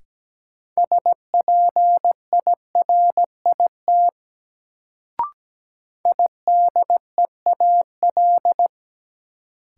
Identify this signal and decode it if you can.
Morse code, heard as a keyed tone, about 17 wpm, 700 Hz. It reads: SPIRIT IDEAL
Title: Sentences from Top 1400 Words - New Word Drill - 17wpm